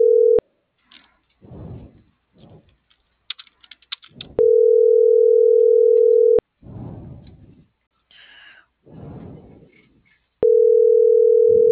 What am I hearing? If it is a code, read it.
Ringback tone (North America), 440 + 480 Hz.